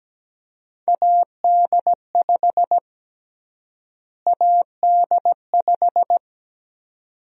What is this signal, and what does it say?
Morse code, heard as a keyed tone, about 17 wpm, 700 Hz. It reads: AD5 AD5
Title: USA Call Sign Prefixes - Repeated 2x - 17wpm